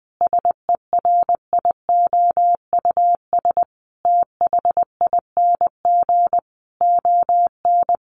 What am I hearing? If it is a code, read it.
Morse code, heard as a keyed tone, about 20 wpm, 700 Hz. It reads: SERIOUS THING ON
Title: Sentences from Top 1100 Words 20wpm